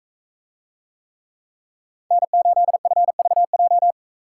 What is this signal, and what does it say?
Morse code, heard as a keyed tone, about 42 wpm, 700 Hz. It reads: N8FVJ